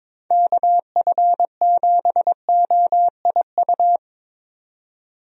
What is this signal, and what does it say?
Morse code, heard as a keyed tone, about 22 wpm, 700 Hz. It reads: KF7OIU
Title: USA Call Signs - 6 Characters - 22wpm